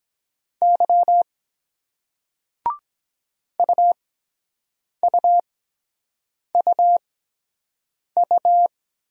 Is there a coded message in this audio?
Yes: Morse code from 0.6 to 8.7 s, heard as a keyed tone, about 23 wpm, spaced as for 7 wpm, 700 Hz. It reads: Y UUUU